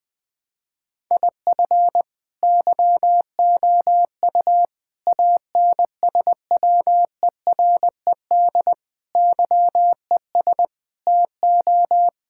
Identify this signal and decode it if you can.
Morse code, heard as a keyed tone, about 20 wpm, 700 Hz. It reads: IF YOU ANSWERED YES TO
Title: Sets of 9 Words - 20wpm